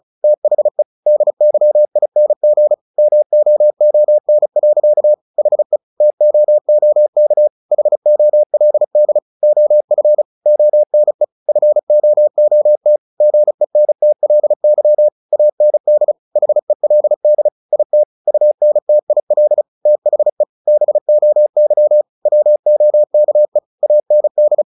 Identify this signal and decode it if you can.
Morse code, heard as a keyed tone, about 34 wpm, 600 Hz. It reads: THE DYING MOON. HE TOOK HOLD OF ONE FOOT GENTLY AND HELD IT UNTIL THE BOY WOKE AND
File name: the old man and the sea / the old man and the sea 1 35wpm